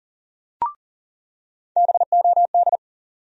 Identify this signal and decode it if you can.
Morse code, heard as a keyed tone, about 37 wpm, 700 Hz. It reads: BOD
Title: USA Call Sign Postfixes - 3 Letters 40wpm